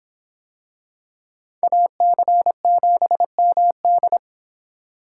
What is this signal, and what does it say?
Morse code, heard as a keyed tone, about 24 wpm, 700 Hz. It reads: AC7MD